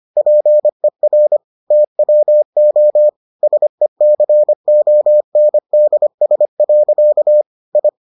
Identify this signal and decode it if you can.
Morse code, heard as a keyed tone, about 25 wpm, 600 Hz. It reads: PER TWO SECONDS. I